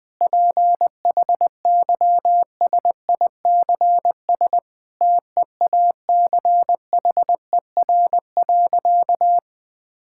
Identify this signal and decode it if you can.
Morse code, heard as a keyed tone, about 20 wpm, 700 Hz. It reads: PHYSICS TEACHER.
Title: Sentences from Top 1000 Words - Review Part 6 - 20wpm